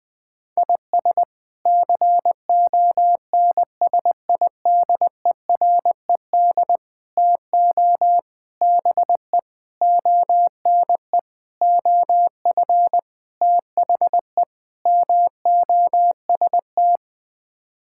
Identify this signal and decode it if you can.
Morse code, heard as a keyed tone, about 20 wpm, 700 Hz. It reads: IS CONSIDERED TO BE ONE OF THE MOST
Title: Sets of 8 Words - 20wpm